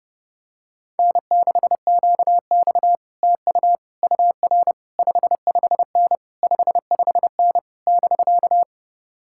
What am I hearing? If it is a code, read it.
Morse code, heard as a keyed tone, about 30 wpm, 700 Hz. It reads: N6QX TU UR 55N 55N <BK>